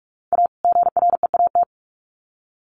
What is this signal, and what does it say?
Morse code, heard as a keyed tone, about 44 wpm, 700 Hz. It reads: A GREAT
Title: Sets of 2 Words 45wpm